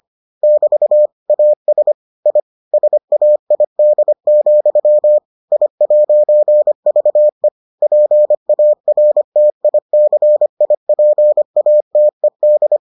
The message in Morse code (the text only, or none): <BT> AS I SAID, I'VE PARTICIPATED